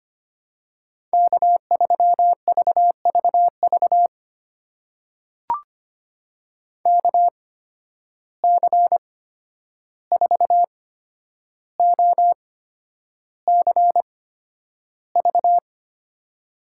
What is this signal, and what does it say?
Morse code, heard as a keyed tone, about 25 wpm, spaced as for 10 wpm, 700 Hz. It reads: K3VVV KC4OCV
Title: USA Call Signs - 8x Character Spacing - 25wpm